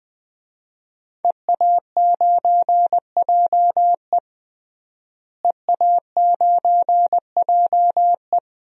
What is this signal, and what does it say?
Morse code, heard as a keyed tone, about 20 wpm, 700 Hz. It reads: EA9JE EA9JE
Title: DXCC-100 Call Signs - 5 Characters - Repeated 2x - 20wpm